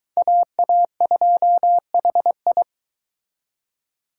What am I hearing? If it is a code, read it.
Morse code, heard as a keyed tone, about 23 wpm, 700 Hz. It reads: AA2HI